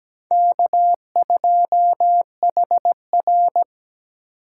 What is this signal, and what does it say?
Morse code, heard as a keyed tone, about 20 wpm, 700 Hz. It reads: K2HR
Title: Speed Racing - USA 1x2 Call Signs 17wpm